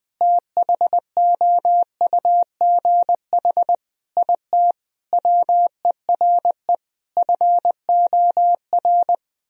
Morse code, heard as a keyed tone, about 20 wpm, 700 Hz. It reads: THOUGH IT WERE FOR
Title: Sentences from Top 1000 Words - Review Part 2 - 20wpm